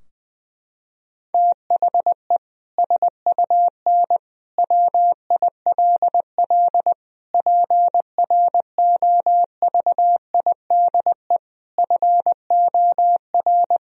Morse code, heard as a keyed tone, about 20 wpm, 700 Hz. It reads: THE SUN WILL PROVIDE FOR
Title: Sentences from Top 1000 Words - Review Part 3 - 20wpm